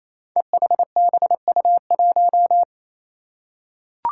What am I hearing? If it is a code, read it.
Morse code, heard as a keyed tone, about 27 wpm, 700 Hz. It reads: EHBU1
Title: Five Challenging Letter-Number Combinations 28wpm